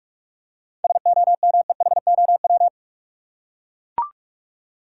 Morse code, heard as a keyed tone, about 44 wpm, 700 Hz. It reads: SOMEHOW
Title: Seven Letter Words - 45wpm